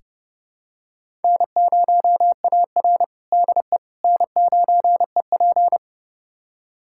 Morse code, heard as a keyed tone, about 30 wpm, 700 Hz. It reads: N0AR DE N9EP
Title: USA Callsign DE Callsign - 4 Characters - 30wpm